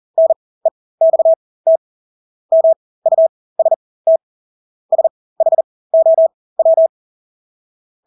Morse code, heard as a keyed tone, about 40 wpm, spaced as for 20 wpm, 650 Hz. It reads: NEXT MUST SHOW
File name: Engelske ord 40-20 wpm